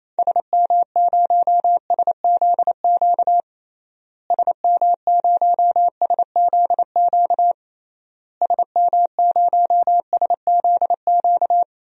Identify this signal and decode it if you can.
Morse code, heard as a keyed tone, about 28 wpm, 700 Hz. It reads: SM0SZQ SM0SZQ SM0SZQ
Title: DXCC-100 Call Signs - 6 Characters - Repeated 3x - 28wpm